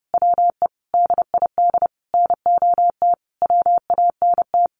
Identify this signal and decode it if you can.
Morse code, heard as a keyed tone, about 30 wpm, 700 Hz. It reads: WE DID NOT WANT
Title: Sentences from Top 1300 Words 30wpm